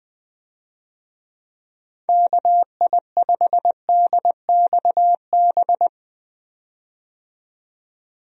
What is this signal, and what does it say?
Morse code, heard as a keyed tone, about 20 wpm, 700 Hz. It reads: KI5DXB
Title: USA Call Signs - 6 Characters - 20wpm